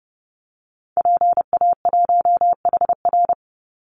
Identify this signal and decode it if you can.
Morse code, heard as a keyed tone, about 30 wpm, 700 Hz. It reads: PA1HR
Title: Call Signs Repeated 2x - 30wpm